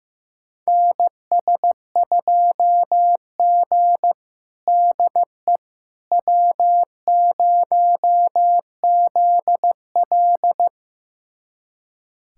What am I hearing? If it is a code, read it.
Morse code, heard as a keyed tone, about 15 wpm, 700 Hz. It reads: NS2G DE W0ZL
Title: USA Callsign DE Callsign - 4 Characters - 15wpm